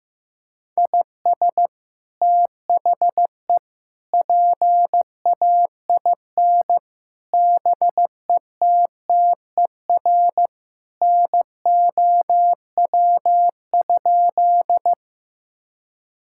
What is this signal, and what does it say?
Morse code, heard as a keyed tone, about 15 wpm, 700 Hz. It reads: IS THE PAIN BETTER NOW?
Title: Sentences from Top 1600 Words 15wpm